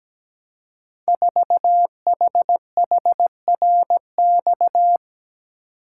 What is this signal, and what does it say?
Morse code, heard as a keyed tone, about 17 wpm, 700 Hz. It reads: 4HHRX
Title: Code Groups - 5 Letter:Numbers - 17wpm